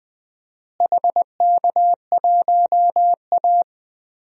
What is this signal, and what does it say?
Morse code, heard as a keyed tone, about 20 wpm, 700 Hz. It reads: HK1A
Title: DXCC-100 Call Signs - 4 Characters - Repeated 2x - 20wpm